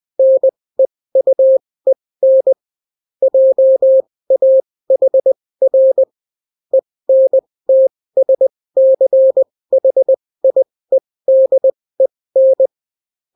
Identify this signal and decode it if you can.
Morse code, heard as a keyed tone, about 20 wpm, spaced as for 16 wpm, 530 Hz. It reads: NEUEN JAHR ENTSCHIEDEN